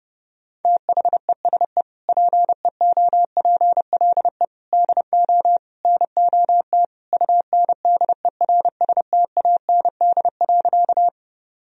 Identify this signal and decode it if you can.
Morse code, heard as a keyed tone, about 30 wpm, 700 Hz. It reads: THESE PEOPLE DO NOT UNDERSTAND.
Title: Sentences from Top 1000 Words - Review Part 1 - 30wpm